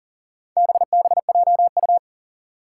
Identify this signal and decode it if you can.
Morse code, heard as a keyed tone, about 39 wpm, 700 Hz. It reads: BBJU
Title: Four Challenging Letter-Number Combinations 40wpm